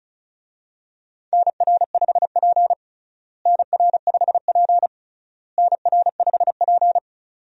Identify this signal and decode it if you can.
Morse code, heard as a keyed tone, about 34 wpm, 700 Hz. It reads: NR5P NR5P NR5P
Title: USA Call Signs - 4 Characters - Repeated 3x - 35wpm